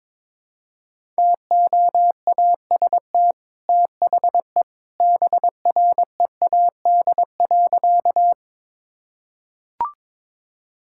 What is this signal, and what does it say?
Morse code, heard as a keyed tone, about 21 wpm, 700 Hz. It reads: TOAST THE BREAD.
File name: practice / 22wpm / Easy Sentences with Hints - 22wpm